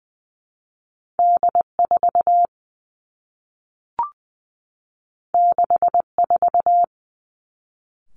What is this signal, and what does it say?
Morse code, heard as a keyed tone, about 20 wpm, 700 Hz. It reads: D 4 6 4
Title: Two Challenging Letter-Number Combinations 20wpm